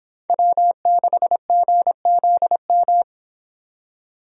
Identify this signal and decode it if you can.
Morse code, heard as a keyed tone, about 26 wpm, 700 Hz. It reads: W6GZM